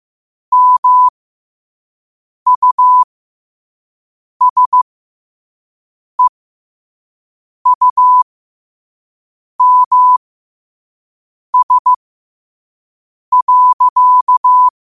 Morse code, heard as a keyed tone, about 15 wpm, spaced as for 5 wpm, 1000 Hz. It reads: MUSEUMS.
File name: qotdfiles / 5wpm